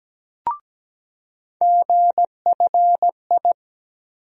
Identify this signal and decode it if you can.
Morse code, heard as a keyed tone, about 17 wpm, 700 Hz. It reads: GFI
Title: USA Call Sign Postfixes - 3 Letters 17wpm